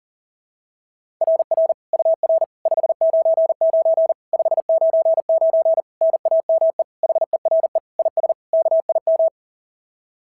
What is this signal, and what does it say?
Morse code, heard as a keyed tone, about 38 wpm, 660 Hz. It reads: RR UR 599 599 NAME HERE IS KIM
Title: SOTA QSO Practice 40wpm